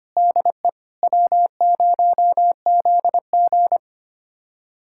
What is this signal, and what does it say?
Morse code, heard as a keyed tone, about 25 wpm, 700 Hz. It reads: DE W0ZG
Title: USA Callsign DE Callsign - 4 Characters - 25wpm